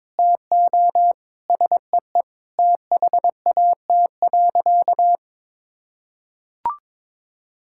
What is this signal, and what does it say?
Morse code, heard as a keyed tone, about 22 wpm, 700 Hz. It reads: TO SEE THAT.
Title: Sentences from Top 1600 Words 22wpm